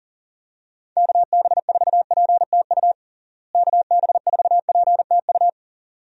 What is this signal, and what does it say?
Morse code, heard as a keyed tone, about 37 wpm, 700 Hz. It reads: KB4PTU KB4PTU
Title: USA Call Signs - 6 Characters - Repeated 2x - 40wpm